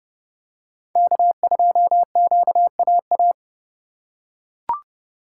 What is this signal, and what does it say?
Morse code, heard as a keyed tone, about 29 wpm, 700 Hz. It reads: K2QAA